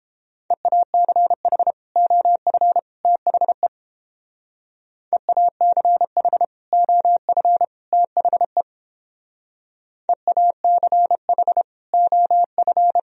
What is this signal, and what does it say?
Morse code, heard as a keyed tone, about 29 wpm, 700 Hz. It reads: EACH OF THE EACH OF THE EACH OF